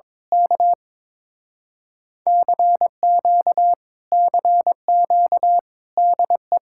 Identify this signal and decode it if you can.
Morse code, heard as a keyed tone, about 22 wpm, 700 Hz. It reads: K CQ CQ DE
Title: Speed-Racing Common QSO Phrases - 22wpm